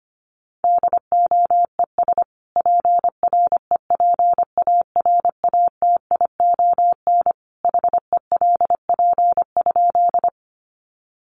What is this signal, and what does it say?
Morse code, heard as a keyed tone, about 25 wpm, 700 Hz. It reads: DOES PREPARATION HELP?
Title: Sentences from Top 1400 Words 25wpm